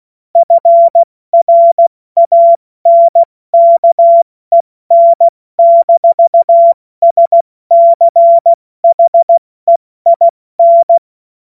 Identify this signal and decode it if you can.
Morse code, heard as a keyed tone, about 16 wpm, 680 Hz. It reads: FRANKEN-SCHEIN
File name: Morsekurs / Witze / Witz 17. August 2021, 70 F80